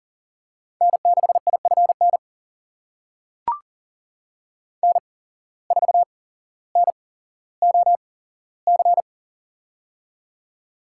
Morse code, heard as a keyed tone, about 40 wpm, spaced as for 17 wpm, 700 Hz. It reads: N6IFN N4NOC
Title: USA Call Signs - 5 Characters - 8x Character Spacing - 40wpm